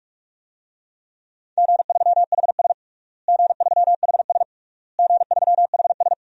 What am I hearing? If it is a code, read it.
Morse code, heard as a keyed tone, about 46 wpm, 700 Hz. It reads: G3HS G3HS G3HS
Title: DXCC-100 Call Signs - 4 Characters - Repeated 3x - 45wpm